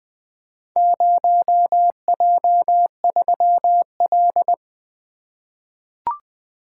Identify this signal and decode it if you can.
Morse code, heard as a keyed tone, about 20 wpm, 700 Hz. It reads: 0J3L
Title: Four Letter-Number Combinations - 20wpm